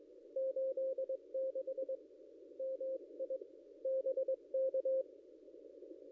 Morse code, heard as a keyed tone, about 22 wpm, 530 Hz. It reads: 86 MI BK
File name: contests / ss-cw-k5tr-op-n5rz -0500